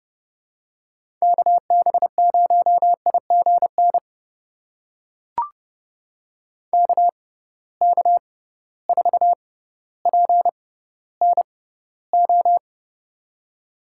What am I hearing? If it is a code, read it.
Morse code, heard as a keyed tone, about 30 wpm, spaced as for 15 wpm, 700 Hz. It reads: KB0IGN KK4PNO
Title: USA Call Signs - 6 Characters - 6x Character Spacing - 30wpm